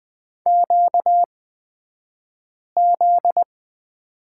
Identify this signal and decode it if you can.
Morse code, heard as a keyed tone, about 20 wpm, spaced as for 5 wpm, 700 Hz. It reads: QZ
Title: 249 Lesson 40d - BK - Two Characters 20wpm